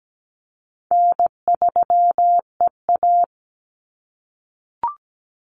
Morse code, heard as a keyed tone, about 17 wpm, 700 Hz. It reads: N3EA
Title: Easy Call Signs - 17wpm